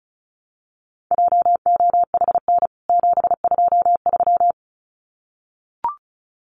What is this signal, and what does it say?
Morse code, heard as a keyed tone, about 34 wpm, 700 Hz. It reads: JOHN 723
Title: CWT Exchange 35wpm